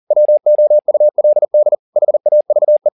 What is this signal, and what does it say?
Morse code, heard as a keyed tone, about 40 wpm, 600 Hz. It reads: WOULD HAVE